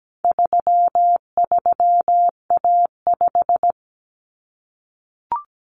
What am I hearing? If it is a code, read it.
Morse code, heard as a keyed tone, about 17 wpm, 700 Hz. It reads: 33A5
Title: Common Cut Numbers - 4-Digit 17wpm